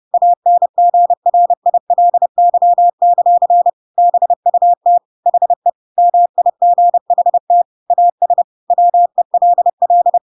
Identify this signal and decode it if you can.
Morse code, heard as a keyed tone, about 30 wpm, 700 Hz. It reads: ANGRILY; BUT HE MIGHT AS WELL
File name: Pinocchio / Pinocchio ch3 30 30 ik1hge